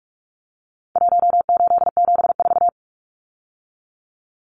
Morse code, heard as a keyed tone, about 44 wpm, 700 Hz. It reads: 1874
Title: Numbers - Contest Style - 4-Digit 45wpm